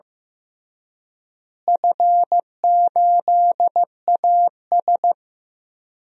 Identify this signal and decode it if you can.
Morse code, heard as a keyed tone, about 15 wpm, 700 Hz. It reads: F8AS
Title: DXCC-100 Call Signs - 4 Characters - Repeated 3x - 15wpm